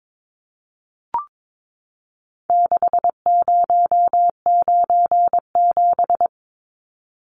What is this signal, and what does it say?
Morse code, heard as a keyed tone, about 22 wpm, 700 Hz. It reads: 6097
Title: Numbers - 4-Digit 22wpm